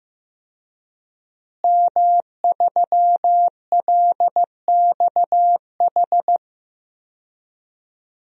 Morse code, heard as a keyed tone, about 15 wpm, 700 Hz. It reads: M3LXH
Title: DXCC-100 Call Signs - 5 Characters - Repeated 2x - 15wpm